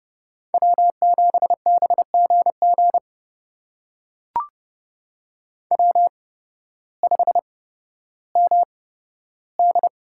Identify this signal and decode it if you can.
Morse code, heard as a keyed tone, about 30 wpm, spaced as for 14 wpm, 700 Hz. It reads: W7BGG W5MD
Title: USA Call Signs - 8x Character Spacing - 30wpm